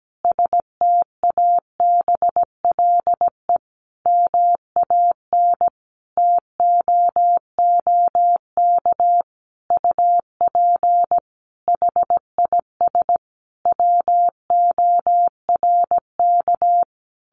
Morse code, heard as a keyed tone, about 17 wpm, 700 Hz. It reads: STABLE MAN TOOK UP HIS WORK
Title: Sentences from Top 1500 Words 17wpm